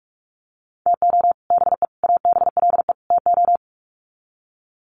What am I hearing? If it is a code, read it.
Morse code, heard as a keyed tone, about 44 wpm, 700 Hz. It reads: TO BE ABLE TO
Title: Sets of 4 Words 45wpm